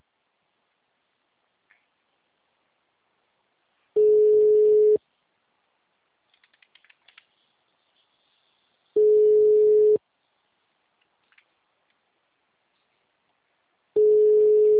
Ringback tone (Europe, 425 Hz).
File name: out- -2027- - - .62672